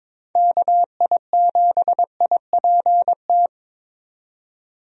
Morse code, heard as a keyed tone, about 21 wpm, 700 Hz. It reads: KI7IPT
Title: USA Call Signs - 6 Characters - 22wpm